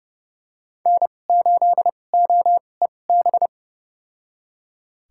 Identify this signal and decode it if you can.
Morse code, heard as a keyed tone, about 30 wpm, spaced as for 22 wpm, 700 Hz. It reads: N8OEB
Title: USA Call Signs - 5 Characters - 2x Character Spacing - 30wpm